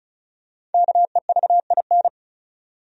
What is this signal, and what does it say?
Morse code, heard as a keyed tone, about 36 wpm, 700 Hz. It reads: KEVIN
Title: USA Common Names - Max 5 Letters - 35wpm